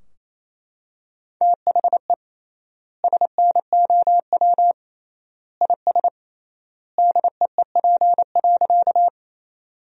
Morse code, heard as a keyed tone, about 28 wpm, 700 Hz. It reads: THE SNOW IS DEEP.
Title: 3x Word-Spacing - Sentences from Top 500 Words - 28wpm